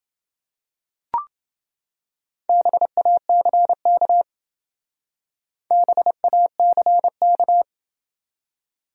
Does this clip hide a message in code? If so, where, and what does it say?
Yes: Morse code from 2.5 to 7.6 s, heard as a keyed tone, about 28 wpm, 700 Hz. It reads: BACK BACK